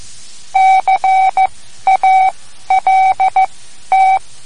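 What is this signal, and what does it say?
Morse code, heard as a keyed tone, about 14 wpm, spaced as for 11 wpm, 760 Hz. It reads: CALT
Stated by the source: Morse code lesson, third lesson in a series for learning the Morse alphabet.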